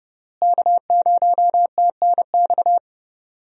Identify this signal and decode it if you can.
Morse code, heard as a keyed tone, about 32 wpm, 700 Hz. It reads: K0TNX
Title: USA Call Signs - 5 Characters - Repeated 3x - 30wpm